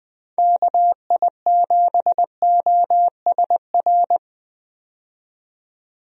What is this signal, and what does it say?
Morse code, heard as a keyed tone, about 20 wpm, 700 Hz. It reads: KI7OSR
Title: USA Call Signs - 6 Characters - 20wpm